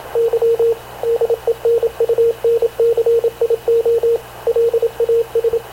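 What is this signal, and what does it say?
Morse code, heard as a keyed tone, about 26 wpm, 460 Hz, with radio noise. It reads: Y DENUNCIO LAS
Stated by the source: Havana Radio CLA press (PX) broadcast, recorded by Radio Officer Paul Dumesnil aboard mv CNAV QUEST/CZDO.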